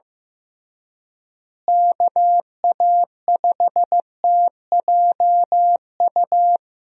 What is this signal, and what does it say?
Morse code, heard as a keyed tone, about 16 wpm, 700 Hz. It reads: KA5TJU
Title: USA Call Signs - 6 Characters - Repeated 2x - 15wpm